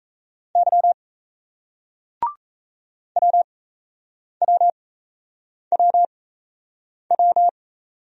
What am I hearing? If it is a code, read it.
Morse code, heard as a keyed tone, about 38 wpm, spaced as for 8 wpm, 700 Hz. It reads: Y WWWW